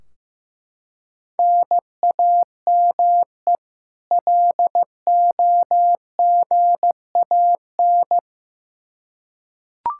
Morse code, heard as a keyed tone, about 15 wpm, 700 Hz. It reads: NAME LOGAN
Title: Practice QSO - Exchange 1 - Beginning - 15wpm